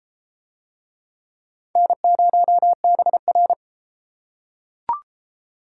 Morse code, heard as a keyed tone, about 33 wpm, 700 Hz. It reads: N0BR